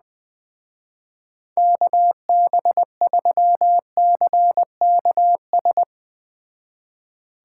Morse code, heard as a keyed tone, about 21 wpm, 700 Hz. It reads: KB3CKS